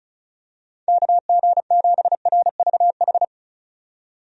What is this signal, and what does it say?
Morse code, heard as a keyed tone, about 34 wpm, 700 Hz. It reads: KG7RVH
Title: USA Call Signs - 6 Characters - 35wpm